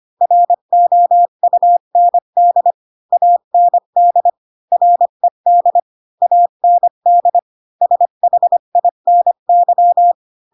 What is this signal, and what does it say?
Morse code, heard as a keyed tone, about 25 wpm, 700 Hz. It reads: ROUND AND RED AND SHINY
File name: Pinocchio / Pinocchio ch1 20 25 ik1hge